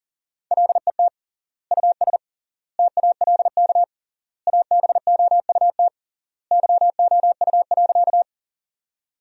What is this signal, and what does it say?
Morse code, heard as a keyed tone, about 37 wpm, 700 Hz. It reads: LET US TALK ABOUT YOU.
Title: 3x Word-Spacing - Sentences from Top 400 Words - 40wpm